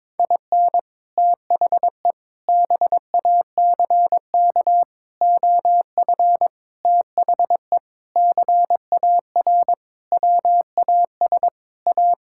Morse code, heard as a keyed tone, about 21 wpm, 700 Hz. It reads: IN THE BACK OF THE CAR WAS A
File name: practice / 22wpm / Sentences with 4-Word Phrases - 22wpm